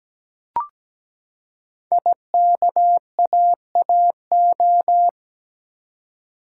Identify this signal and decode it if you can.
Morse code, heard as a keyed tone, about 17 wpm, 700 Hz. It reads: IKAAO
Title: Code Groups - 5 Letters - 17wpm